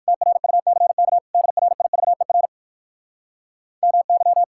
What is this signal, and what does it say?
Morse code, heard as a keyed tone, about 62 wpm, 700 Hz. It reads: TRUCK DRIVER MY